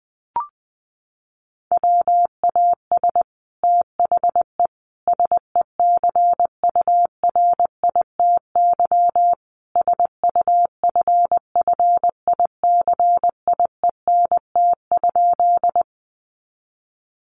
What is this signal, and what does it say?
Morse code, heard as a keyed tone, about 20 wpm, 700 Hz. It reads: WAS THE SECURITY SUFFICIENT?
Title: Sentences from Top 1400 Words 20wpm